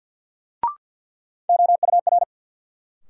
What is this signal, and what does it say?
Morse code, heard as a keyed tone, about 53 wpm, 700 Hz. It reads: OUR
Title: Sets of 2 Words - New Word Drill - 50wpm